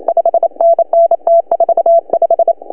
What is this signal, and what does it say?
Morse code, heard as a keyed tone, about 33 wpm, 690 Hz. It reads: 5NN T45
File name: CW / HA2OS RHOB 2017 CW